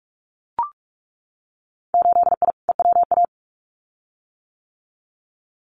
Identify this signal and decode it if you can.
Morse code, heard as a keyed tone, about 44 wpm, 700 Hz. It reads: 8I EWA
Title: Winter Field Day Exchanges Encore 45wpm